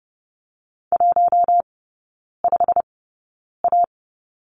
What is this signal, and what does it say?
Morse code, heard as a keyed tone, about 30 wpm, spaced as for 9 wpm, 700 Hz. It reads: 15A
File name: Lesson 16e - Letter c - Three Characters 30wpm